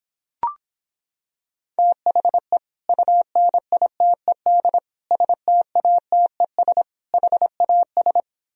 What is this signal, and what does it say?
Morse code, heard as a keyed tone, about 26 wpm, 700 Hz. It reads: THE UNITED STATES HAS